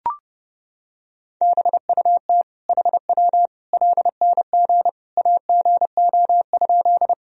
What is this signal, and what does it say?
Morse code, heard as a keyed tone, about 30 wpm, 700 Hz. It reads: BUT HW LNG AGO?
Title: Sentences from Top 500 Words and Abbreviations 30wpm